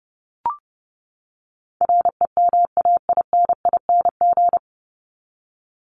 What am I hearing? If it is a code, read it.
Morse code, heard as a keyed tone, about 30 wpm, 700 Hz. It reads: REMAINING
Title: ING Suffix - Top 100 Words with Base Words 30wpm